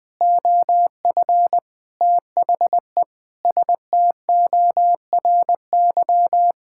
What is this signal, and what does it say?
Morse code, heard as a keyed tone, about 21 wpm, 700 Hz. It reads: OF THE STORY
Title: Sentences with 5-Word Phrases - 20wpm